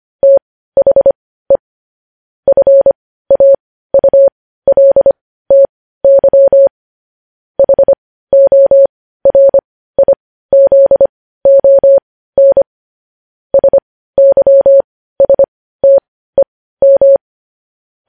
Morse code, heard as a keyed tone, about 25 wpm, spaced as for 15 wpm, 550 Hz. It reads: THE FAULTY HORIZON SYSTEM
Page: news-headlines-in-morse-code-at-15-wpm